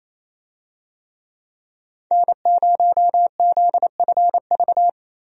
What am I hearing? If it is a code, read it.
Morse code, heard as a keyed tone, about 28 wpm, 700 Hz. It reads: N0ZFV
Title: USA Call Signs - 5 Characters - Repeated 2x - 28wpm